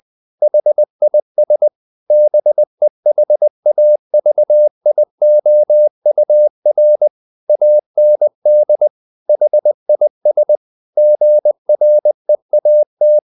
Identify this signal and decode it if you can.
Morse code, heard as a keyed tone, about 20 wpm, 600 Hz. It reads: HIS BEHAVIOUR AND HIS GREAT
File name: the old man and the sea part 2 20wpm